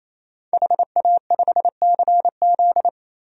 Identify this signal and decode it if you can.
Morse code, heard as a keyed tone, about 28 wpm, 700 Hz. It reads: HA5CZ
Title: DXCC-100 Call Signs - 5 Characters - 28wpm